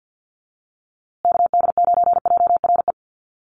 Morse code, heard as a keyed tone, about 50 wpm, 700 Hz. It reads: KD9JRE
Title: USA Call Signs Repeated 2x 50wpm